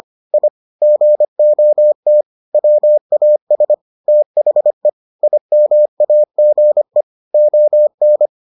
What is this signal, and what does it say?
Morse code, heard as a keyed tone, about 25 wpm, 600 Hz. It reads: I GOT WAS THE IMAGE ON